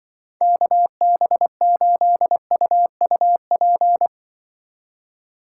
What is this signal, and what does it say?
Morse code, heard as a keyed tone, about 24 wpm, 700 Hz. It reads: KB8UUP
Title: Speed Racing - USA 2x3 Call Signs 20wpm